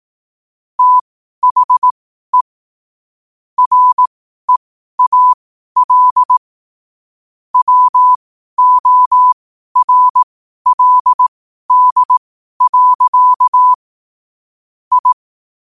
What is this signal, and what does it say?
Morse code, heard as a keyed tone, about 18 wpm, spaced as for 12 wpm, 1000 Hz. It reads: THE REAL WORLD. I
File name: qotdfiles / 12wpm